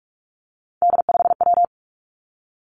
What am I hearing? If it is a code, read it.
Morse code, heard as a keyed tone, about 46 wpm, 700 Hz. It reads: D5W
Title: Three Challenging Letter-Number Combinations 45wpm